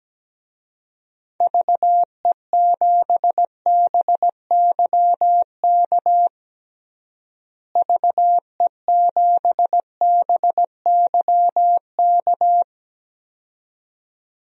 Morse code, heard as a keyed tone, about 17 wpm, 700 Hz. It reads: VE7BYK VE7BYK
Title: DXCC-100 Call Signs - 6 Characters - Repeated 2x - 17wpm